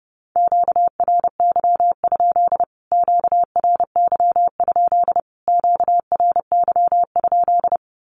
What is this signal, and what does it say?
Morse code, heard as a keyed tone, about 30 wpm, 700 Hz. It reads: QRY? QRY? QRY?
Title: Q-Codes Repeated 3x 30wpm